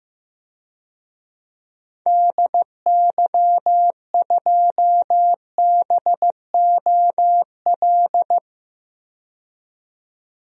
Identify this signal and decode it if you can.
Morse code, heard as a keyed tone, about 16 wpm, 700 Hz. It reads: DY2BOL
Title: DXCC-100 Call Signs - 6 Characters - 15wpm